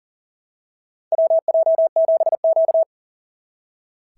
Morse code, heard as a keyed tone, about 39 wpm, 650 Hz. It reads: WJ7Q
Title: SOTA QSO Practice 40wpm